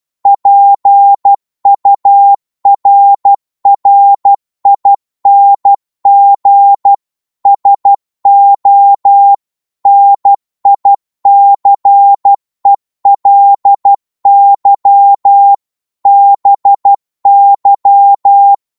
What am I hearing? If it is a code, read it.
Morse code, heard as a keyed tone, about 12 wpm, 800 Hz. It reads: PURRING SO NICELY BY
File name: Morse / psg-12wpm